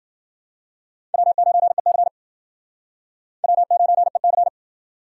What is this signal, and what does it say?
Morse code, heard as a keyed tone, about 52 wpm, 700 Hz. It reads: W9EC W9EC